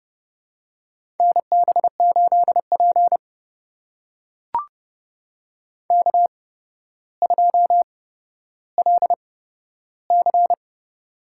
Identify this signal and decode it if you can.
Morse code, heard as a keyed tone, about 32 wpm, spaced as for 13 wpm, 700 Hz. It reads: NB8P K2LC
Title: USA Call Signs - 4 Characters - 8x Character Spacing - 30wpm